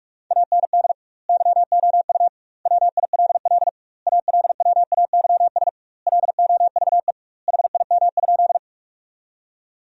Morse code, heard as a keyed tone, about 43 wpm, 700 Hz. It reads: AND YOU WILL ALWAYS LOVE HIM?
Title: 2x Word-Spacing - Sentences from Top 400 Words - 45wpm